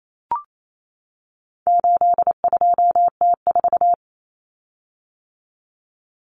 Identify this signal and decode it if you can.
Morse code, heard as a keyed tone, about 28 wpm, 700 Hz. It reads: 82T4
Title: Common Cut Numbers - 4-Digit 28wpm